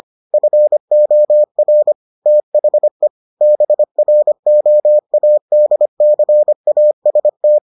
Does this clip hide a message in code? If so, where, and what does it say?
Yes: Morse code from 0.3 to 7.6 s, heard as a keyed tone, about 25 wpm, 600 Hz. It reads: FOR THE BROADCAST